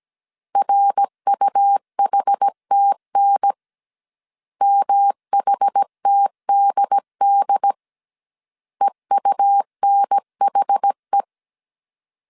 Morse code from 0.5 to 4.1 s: RUHTN